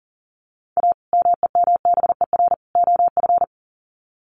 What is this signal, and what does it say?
Morse code, heard as a keyed tone, about 40 wpm, 700 Hz. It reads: A MEMBER OF
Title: Sets of 3 Words 40wpm